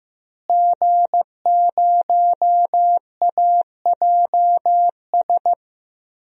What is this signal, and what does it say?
Morse code, heard as a keyed tone, about 15 wpm, 700 Hz. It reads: G0AJS
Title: DXCC-100 Call Signs - 5 Characters - Repeated 3x - 15wpm